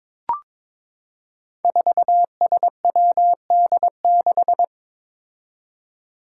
Morse code, heard as a keyed tone, about 22 wpm, 700 Hz. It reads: 4SWD6
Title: Five Challenging Letter-Number Combinations 22wpm